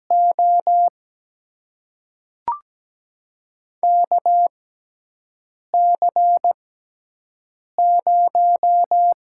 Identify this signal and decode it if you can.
Morse code, heard as a keyed tone, about 17 wpm, spaced as for 6 wpm, 700 Hz. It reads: O KC0
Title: USA Call Signs - 6 Characters - 6x Character Spacing - 17wpm